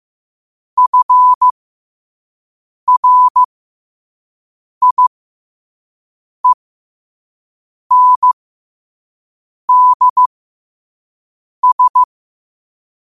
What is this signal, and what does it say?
Morse code, heard as a keyed tone, about 15 wpm, spaced as for 5 wpm, 1000 Hz. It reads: FRIENDS